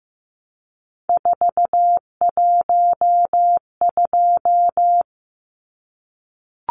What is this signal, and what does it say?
Morse code, heard as a keyed tone, about 15 wpm, 700 Hz. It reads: 412
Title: Numbers - 3-Digit 15wpm